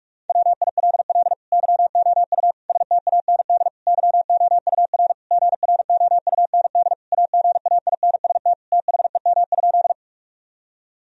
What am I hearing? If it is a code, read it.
Morse code, heard as a keyed tone, about 43 wpm, 700 Hz. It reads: WILL YOU STAND YOUR GROUND AGAINST THEM?
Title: Sentences from Top 1000 Words - Review Part 2 - 45wpm